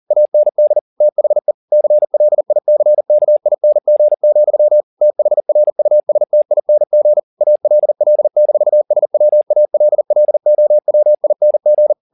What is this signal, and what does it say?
Morse code, heard as a keyed tone, about 40 wpm, 600 Hz. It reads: AND THE CLICKING, THRUSTING ALL-SWALLOWING
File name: the old man and the sea 2 40wpm